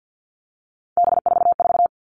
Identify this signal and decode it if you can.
Morse code, heard as a keyed tone, about 50 wpm, 700 Hz. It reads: B44
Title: All Cut Numbers - 3-Digit 50wpm